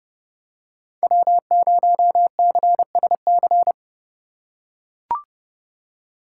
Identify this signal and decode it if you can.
Morse code, heard as a keyed tone, about 30 wpm, 700 Hz. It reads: W0CSC